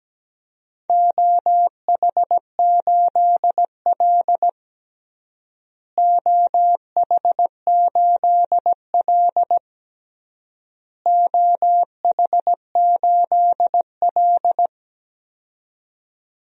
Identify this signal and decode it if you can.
Morse code, heard as a keyed tone, about 17 wpm, 700 Hz. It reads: OH8L OH8L OH8L
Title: DXCC-100 Call Signs - 4 Characters - Repeated 3x - 17wpm